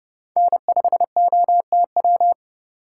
Morse code, heard as a keyed tone, about 30 wpm, 700 Hz. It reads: N5OTW